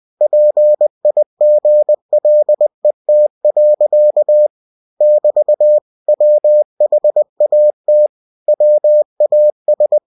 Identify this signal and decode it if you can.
Morse code, heard as a keyed tone, about 20 wpm, 600 Hz. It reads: PIGLET. <BT> WHAT WAS
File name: cw-books / winnie-the-pooh / winnie-20wpm-page-0033